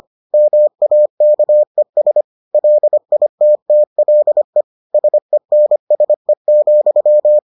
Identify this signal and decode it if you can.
Morse code, heard as a keyed tone, about 25 wpm, 600 Hz. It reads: MAKES LITTLE SENSE,